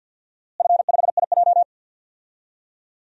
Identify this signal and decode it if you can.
Morse code, heard as a keyed tone, about 48 wpm, 700 Hz. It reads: F5IJ
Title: DXCC-100 Call Signs - 4 Characters - Repeated 2x - 50wpm